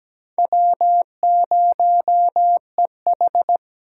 Morse code, heard as a keyed tone, about 20 wpm, 700 Hz. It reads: W0EH
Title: Speed Racing - USA 1x2 Call Signs 17wpm